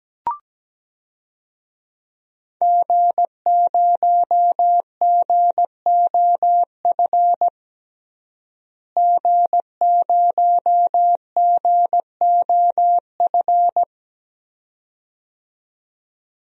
Morse code, heard as a keyed tone, about 17 wpm, 700 Hz. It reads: G0GOF G0GOF
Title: DXCC-100 Call Signs - 5 Characters - Repeated 2x - 17wpm